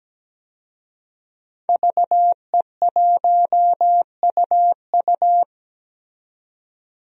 Morse code, heard as a keyed tone, about 17 wpm, 700 Hz. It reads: VE1UU
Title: DXCC-100 Call Signs - 5 Characters - 17wpm